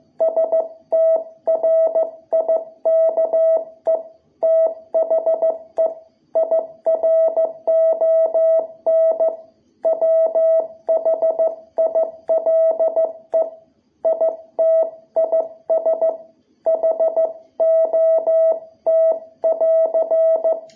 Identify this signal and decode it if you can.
Morse code, heard as a keyed tone, about 15 wpm, 660 Hz. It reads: STRIKE THE IRON WHILE IT IS HOT <AR>
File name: jh8cbh / 2021cw